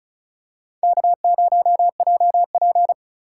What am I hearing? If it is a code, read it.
Morse code, heard as a keyed tone, about 34 wpm, 700 Hz. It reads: K0JP
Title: USA Call Signs - 4 Characters - 3x Character Spacing - 35wpm